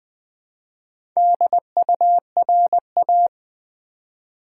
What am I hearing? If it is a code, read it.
Morse code, heard as a keyed tone, about 20 wpm, 700 Hz. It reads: DURA
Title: Build A Word - 2 Letters At A Time - 8-Letter Words - 20wpm